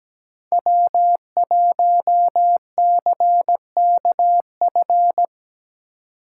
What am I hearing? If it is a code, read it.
Morse code, heard as a keyed tone, about 20 wpm, 700 Hz. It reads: W1CKF
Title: Speed Racing - USA 1x3 Call Signs 17wpm